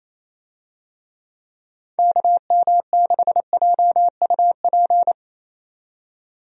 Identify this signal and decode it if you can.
Morse code, heard as a keyed tone, about 28 wpm, 700 Hz. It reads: KM6JUP
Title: USA Call Signs - 6 Characters - 28wpm